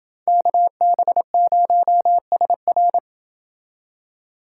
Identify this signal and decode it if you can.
Morse code, heard as a keyed tone, about 29 wpm, 700 Hz. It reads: KB0SR